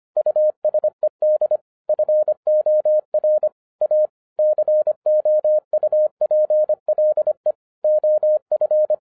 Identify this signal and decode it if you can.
Morse code, heard as a keyed tone, about 25 wpm, 610 Hz. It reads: USED FOR A COUPLE OF
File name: FIRST FISTS CONTACT-25wpm